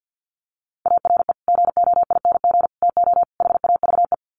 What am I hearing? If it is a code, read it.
Morse code, heard as a keyed tone, about 50 wpm, 700 Hz. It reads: ARE GOING TO HAVE
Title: Sets of 4 Words 50wpm